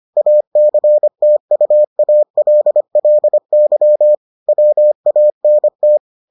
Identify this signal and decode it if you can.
Morse code, heard as a keyed tone, about 25 wpm, 600 Hz. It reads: ACTUALLY WANT